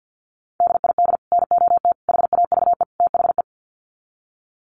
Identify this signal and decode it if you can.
Morse code, heard as a keyed tone, about 50 wpm, 700 Hz. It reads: DID NOT HAVE THE
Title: Sets of 4 Words 50wpm